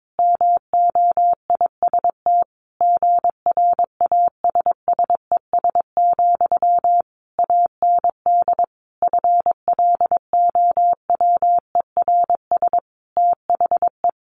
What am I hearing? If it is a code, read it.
Morse code, heard as a keyed tone, about 22 wpm, 700 Hz. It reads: MOIST GRASSES, AND FLOWERS THE
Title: A Sound Of Thunder 22wpm